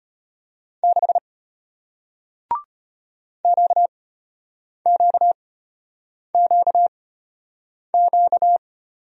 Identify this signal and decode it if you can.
Morse code, heard as a keyed tone, about 34 wpm, spaced as for 8 wpm, 690 Hz. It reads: 6 QQQQ